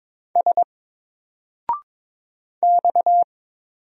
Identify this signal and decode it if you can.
Morse code, heard as a keyed tone, about 22 wpm, 700 Hz. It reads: S X
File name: Single-Letter-Number-22wpm